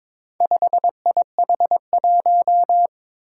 Morse code, heard as a keyed tone, about 22 wpm, 700 Hz. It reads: 5IH1
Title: Four Challenging Letter-Number Combinations 22wpm